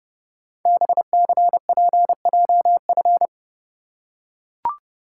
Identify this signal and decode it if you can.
Morse code, heard as a keyed tone, about 30 wpm, 700 Hz. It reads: BCPJF
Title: Code Groups - 5 Letters - 30wpm